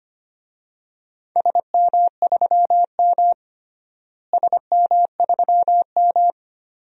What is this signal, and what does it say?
Morse code, heard as a keyed tone, about 25 wpm, 700 Hz. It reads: SM3M SM3M
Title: DXCC-100 Call Signs - 4 Characters - Repeated 2x - 25wpm